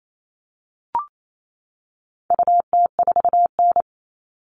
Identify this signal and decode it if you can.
Morse code, heard as a keyed tone, about 28 wpm, 700 Hz. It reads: UT4N
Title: All Cut Numbers - 4-Digit 28wpm